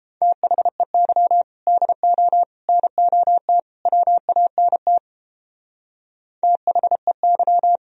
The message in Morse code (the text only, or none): THEY DO NOT WANT THEY